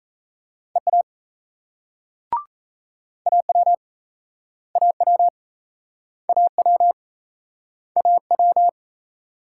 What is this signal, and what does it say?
Morse code, heard as a keyed tone, about 38 wpm, 700 Hz. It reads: EA AW AW AW AW